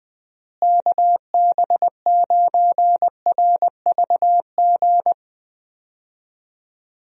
Morse code, heard as a keyed tone, about 20 wpm, 700 Hz. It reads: KB9RVG
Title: USA Call Signs - 6 Characters - Repeated 2x - 20wpm